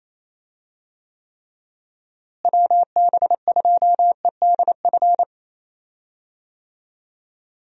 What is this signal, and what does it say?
Morse code, heard as a keyed tone, about 28 wpm, 700 Hz. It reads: WB2EDF
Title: USA Call Signs - 6 Characters - Repeated 2x - 28wpm